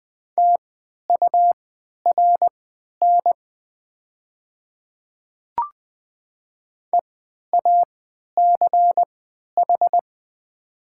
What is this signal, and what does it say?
Morse code, heard as a keyed tone, about 20 wpm, spaced as for 11 wpm, 700 Hz. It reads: TURN EACH
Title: 3x Character Spacing - Top 200 Words - 20wpm